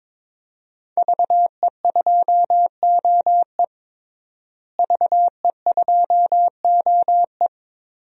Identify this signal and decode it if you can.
Morse code, heard as a keyed tone, about 21 wpm, 700 Hz. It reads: VE2OE VE2OE
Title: DXCC-100 Call Signs - 5 Characters - Repeated 2x - 22wpm